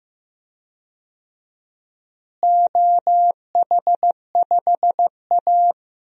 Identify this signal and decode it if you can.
Morse code, heard as a keyed tone, about 15 wpm, 700 Hz. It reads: OH5A
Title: DXCC-100 Call Signs - 4 Characters - Repeated 3x - 15wpm